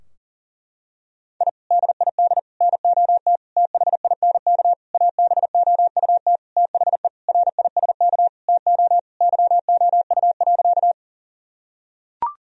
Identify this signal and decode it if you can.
Morse code, heard as a keyed tone, about 40 wpm, 700 Hz. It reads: I DID NOT THINK ABOUT THE RISK TO YOU.
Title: Sentences from Top 1000 Words - Review Part 4 - 40wpm